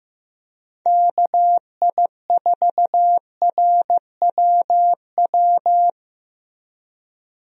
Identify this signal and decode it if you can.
Morse code, heard as a keyed tone, about 15 wpm, 700 Hz. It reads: KI4RWW
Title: USA Call Signs - 6 Characters - Repeated 2x - 15wpm